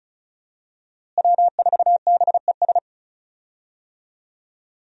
Morse code, heard as a keyed tone, about 36 wpm, 700 Hz. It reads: W4BES
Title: USA Call Signs - 5 Characters - Repeated 2x - 35wpm